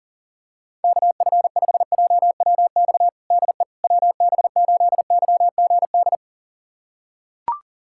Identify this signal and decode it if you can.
Morse code, heard as a keyed tone, about 40 wpm, 700 Hz. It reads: KF5JWX DE WB8YGD
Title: USA Callsign DE Callsign 40wpm